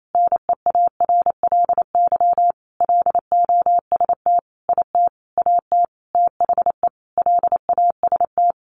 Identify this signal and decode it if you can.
Morse code, heard as a keyed tone, about 28 wpm, 700 Hz. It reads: NEARLY LOST IT AT THE LAST
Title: Sentences from Top 1400 Words 28wpm